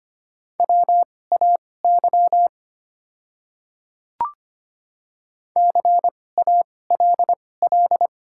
Morse code, heard as a keyed tone, about 25 wpm, spaced as for 18 wpm, 700 Hz. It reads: WAY CALL
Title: 2x Character Spacing - Top 100 Words - 25wpm